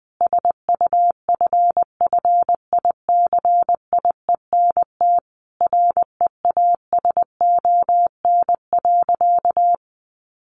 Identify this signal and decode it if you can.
Morse code, heard as a keyed tone, about 20 wpm, 700 Hz. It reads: SUFFICIENT REASON.
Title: Sentences from Top 1400 Words 20wpm